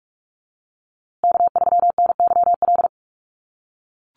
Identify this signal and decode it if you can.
Morse code, heard as a keyed tone, about 44 wpm, 700 Hz. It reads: K3NYL
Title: USA Call Signs Repeated 2x 45wpm